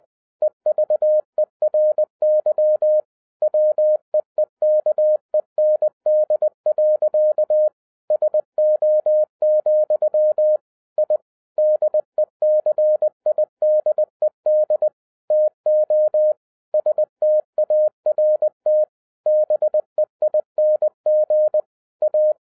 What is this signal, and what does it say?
Morse code, heard as a keyed tone, about 20 wpm, 610 Hz. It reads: EVERY WEEKEND. SO, I DECIDED TO START BEING A
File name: 2012-01-mp3 / n16-THROUGH EIGHT MONTHS-20wpm